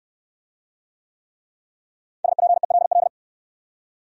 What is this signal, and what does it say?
Morse code, heard as a keyed tone, about 61 wpm, 700 Hz. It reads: SPELL